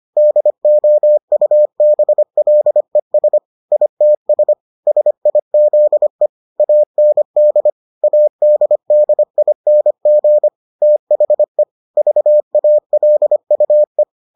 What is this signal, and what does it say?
Morse code, heard as a keyed tone, about 25 wpm, 600 Hz. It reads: DOUBLES ITS SIZE AND ADDING THE VALUE